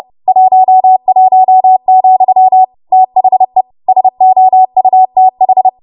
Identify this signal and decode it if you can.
Morse code, heard as a keyed tone, about 30 wpm, 750 Hz. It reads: 11, THE SOUTH
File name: Morse / 30WPM